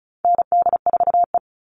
Morse code, heard as a keyed tone, about 34 wpm, 700 Hz. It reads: ND4E
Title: All Cut Numbers - 4-Digit 35wpm